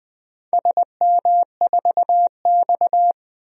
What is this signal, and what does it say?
Morse code, heard as a keyed tone, about 20 wpm, 700 Hz. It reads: SM4X
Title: DXCC-100 Call Signs - 4 Characters - Repeated 3x - 20wpm